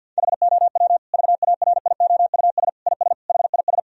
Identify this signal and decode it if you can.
Morse code, heard as a keyed tone, about 50 wpm, 700 Hz. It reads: HOW VARIOUS IS HIS
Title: Sentences from Top 1000 Words - Review Part 4 - 50wpm